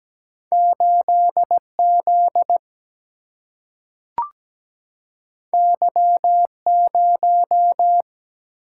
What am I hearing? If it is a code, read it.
Morse code, heard as a keyed tone, about 17 wpm, 700 Hz. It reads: 8Z Y0
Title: Single-Letter Single-Number 17wpm